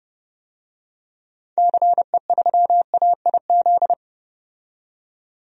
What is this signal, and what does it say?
Morse code, heard as a keyed tone, about 30 wpm, 700 Hz. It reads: CE3AIZ
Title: DXCC-100 Call Signs - 6 Characters - 30wpm